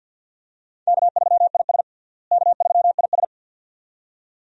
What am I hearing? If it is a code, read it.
Morse code, heard as a keyed tone, about 47 wpm, 700 Hz. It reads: K3IS K3IS
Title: USA Call Signs - 4 Characters - Repeated 2x - 50wpm